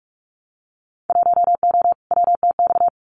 Morse code, heard as a keyed tone, about 46 wpm, 700 Hz. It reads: 1O WTX
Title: Winter Field Day Exchanges Encore 45wpm